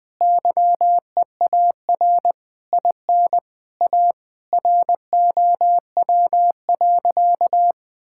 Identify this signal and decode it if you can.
Morse code, heard as a keyed tone, about 21 wpm, 700 Hz. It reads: YEAR IN A ROW.
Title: Sentences with 7-Word Phrases - 20wpm